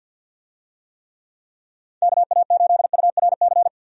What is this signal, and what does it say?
Morse code, heard as a keyed tone, about 50 wpm, 700 Hz. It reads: KA8URC
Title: USA Call Signs - 6 Characters - 50wpm